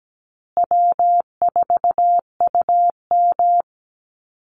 Morse code, heard as a keyed tone, about 17 wpm, 700 Hz. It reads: W4UM
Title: Easy Call Signs Repeated 3x - 17wpm